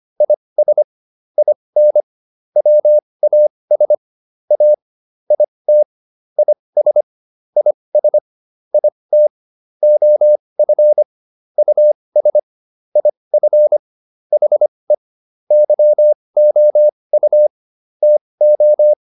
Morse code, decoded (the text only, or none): IS IN WAS A IT IS IS IT OF US IF HE YOU TO